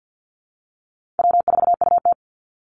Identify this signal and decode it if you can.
Morse code, heard as a keyed tone, about 50 wpm, 700 Hz. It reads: W4UT
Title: Easy Call Signs - 50wpm